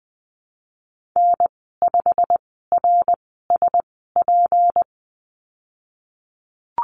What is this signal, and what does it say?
Morse code, heard as a keyed tone, about 20 wpm, spaced as for 15 wpm, 700 Hz. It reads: N5RSP
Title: 154 Lesson 25g - Letter p - Call Signs 20wpm